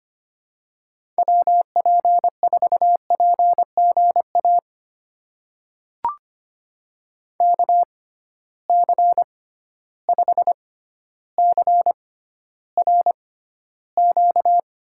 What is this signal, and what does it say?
Morse code, heard as a keyed tone, about 25 wpm, spaced as for 13 wpm, 700 Hz. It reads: WP4PGA KC5CRQ
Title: USA Call Signs - 6 Characters - 6x Character Spacing - 25wpm